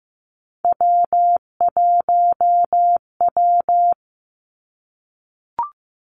Morse code, heard as a keyed tone, about 15 wpm, 700 Hz. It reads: W1W
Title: Three Challenging Letter-Number Combinations 15wpm